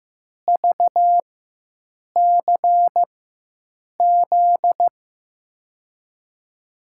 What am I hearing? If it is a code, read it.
Morse code, heard as a keyed tone, about 15 wpm, spaced as for 7 wpm, 700 Hz. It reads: VCZ
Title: USA Call Signs - 4x Character Spacing - 15wpm